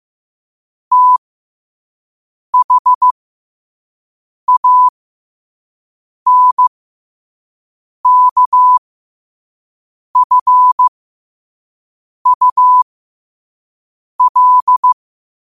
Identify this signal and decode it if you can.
Morse code, heard as a keyed tone, about 15 wpm, spaced as for 5 wpm, 1000 Hz. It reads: THANKFUL
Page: Quotes for Fri, 15 Aug 2025 in Morse Code at 5 words per minute.